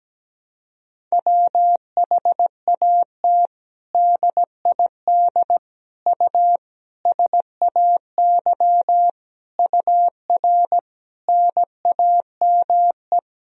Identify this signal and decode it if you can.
Morse code, heard as a keyed tone, about 17 wpm, 700 Hz. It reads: WHAT DID U SAY UR NAME
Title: Sentences from Top 200 Words and Abbreviations 17wpm